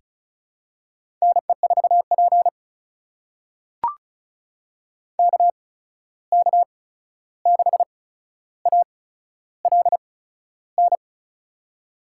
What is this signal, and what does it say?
Morse code, heard as a keyed tone, about 36 wpm, spaced as for 13 wpm, 700 Hz. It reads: NE4P KK6ALN
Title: USA Call Signs - 8x Character Spacing - 35wpm